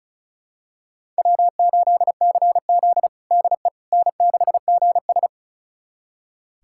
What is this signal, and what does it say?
Morse code, heard as a keyed tone, about 36 wpm, 700 Hz. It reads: W8CZ DE N6GS
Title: USA Callsign DE Callsign - 4 Characters - 35wpm